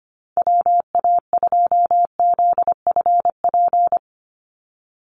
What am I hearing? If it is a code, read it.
Morse code, heard as a keyed tone, about 25 wpm, 700 Hz. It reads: WA2ZFP
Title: USA Call Signs 25wpm